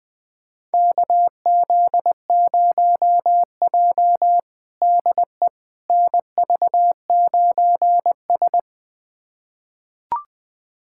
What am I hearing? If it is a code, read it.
Morse code, heard as a keyed tone, about 20 wpm, 700 Hz. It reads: KZ0J DE NV9S
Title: USA Callsign DE Callsign - 4 Characters - 20wpm